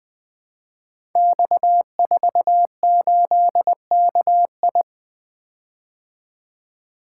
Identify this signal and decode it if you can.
Morse code, heard as a keyed tone, about 20 wpm, 700 Hz. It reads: X48KI
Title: Code Groups - 5 Letter:Numbers - 20wpm